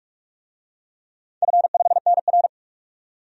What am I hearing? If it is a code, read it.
Morse code, heard as a keyed tone, about 44 wpm, 700 Hz. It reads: F5NR
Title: DXCC-100 Call Signs - 4 Characters - 45wpm